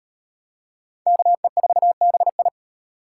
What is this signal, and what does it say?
Morse code, heard as a keyed tone, about 37 wpm, 700 Hz. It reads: KE4BI